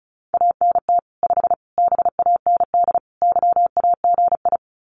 Morse code, heard as a keyed tone, about 34 wpm, 700 Hz. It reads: ANT 5 BAND YAGI
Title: Common QSO Phrases 35wpm